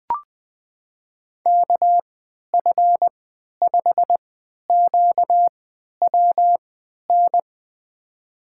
Morse code, heard as a keyed tone, about 20 wpm, spaced as for 11 wpm, 700 Hz. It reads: KF5QWN
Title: USA Call Signs - 6 Characters - 3x Character Spacing - 20wpm